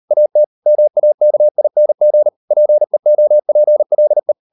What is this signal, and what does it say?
Morse code, heard as a keyed tone, about 38 wpm, 600 Hz. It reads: AT MAKING PEOPLE